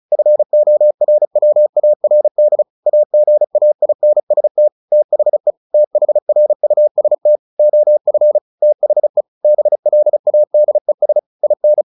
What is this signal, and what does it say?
Morse code, heard as a keyed tone, about 34 wpm, 600 Hz. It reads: FORWARD AGAINST THE THRUST OF THE BLADES IN
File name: the old man and the sea / the old man and the sea 1 35wpm